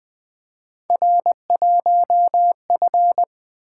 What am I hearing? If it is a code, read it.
Morse code, heard as a keyed tone, about 20 wpm, 700 Hz. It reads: R1F
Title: Three Letter-Number Combinations - 20wpm